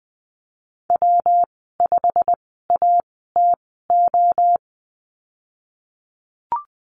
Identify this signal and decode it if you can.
Morse code, heard as a keyed tone, about 20 wpm, spaced as for 15 wpm, 700 Hz. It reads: W5ATO
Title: 122 Lesson 20g - Number 3 - Call Signs 20wpm